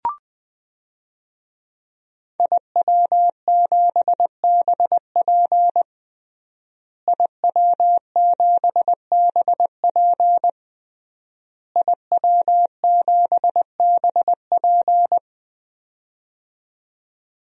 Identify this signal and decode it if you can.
Morse code, heard as a keyed tone, about 20 wpm, 700 Hz. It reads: IW7BP IW7BP IW7BP
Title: DXCC-100 Call Signs - 5 Characters - Repeated 3x - 20wpm